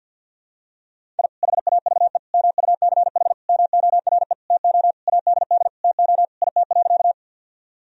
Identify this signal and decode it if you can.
Morse code, heard as a keyed tone, about 52 wpm, 700 Hz. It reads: I HAVE MUCH MORE TO ADD TO IT.
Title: Sentences from Top 1000 Words - Review Part 1 - 50wpm